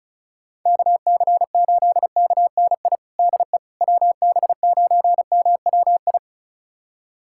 Morse code, heard as a keyed tone, about 36 wpm, 700 Hz. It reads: KC8KNI DE WB9MWI
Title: USA Callsign DE Callsign 35wpm